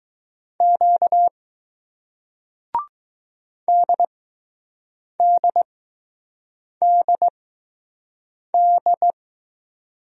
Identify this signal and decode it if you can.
Morse code, heard as a keyed tone, about 20 wpm, spaced as for 6 wpm, 700 Hz. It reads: Q DDDD